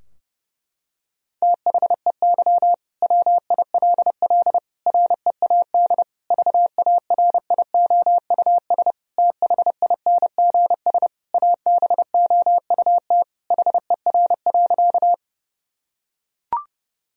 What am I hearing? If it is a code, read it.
Morse code, heard as a keyed tone, about 30 wpm, 700 Hz. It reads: THEY WILL READ VARIOUS THINGS ABOUT HER.
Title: Sentences from Top 1000 Words - Review Part 4 - 30wpm